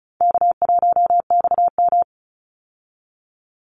Morse code, heard as a keyed tone, about 34 wpm, 700 Hz. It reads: K1XM
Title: Easy Call Signs Repeated 2x - 35wpm